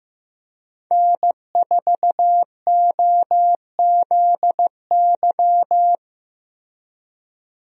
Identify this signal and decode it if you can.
Morse code, heard as a keyed tone, about 15 wpm, 700 Hz. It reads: N4OZY
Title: USA Call Signs - 5 Characters - 15wpm